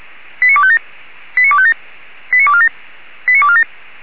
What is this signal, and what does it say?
Five-tone selective call (ZVEI), heard as quick runs of five tones, tones of about 80 ms. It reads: 89257 89257 89247 89247